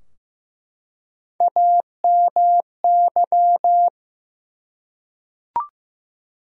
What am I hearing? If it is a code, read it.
Morse code, heard as a keyed tone, about 15 wpm, 700 Hz. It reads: AMY
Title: USA Common Names - Max 4 Letters - 15wpm